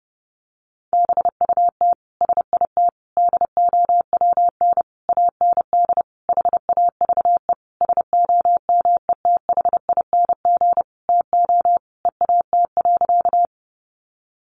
Morse code, heard as a keyed tone, about 30 wpm, 700 Hz. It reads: BUT SIT DOWN AND HAVE SOMETHING TO EAT.
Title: Sentences from Top 600 Words 30wpm